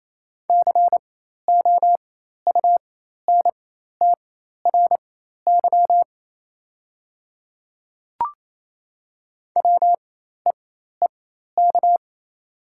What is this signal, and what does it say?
Morse code, heard as a keyed tone, about 28 wpm, spaced as for 13 wpm, 700 Hz. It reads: COUNTRY WEEK
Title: 4x Character Spacing - Top 500 Words - 28wpm